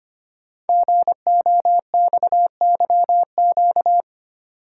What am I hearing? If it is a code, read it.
Morse code, heard as a keyed tone, about 25 wpm, 700 Hz. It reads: GOXYQ
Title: Code Groups - 5 Letter:Numbers - 25wpm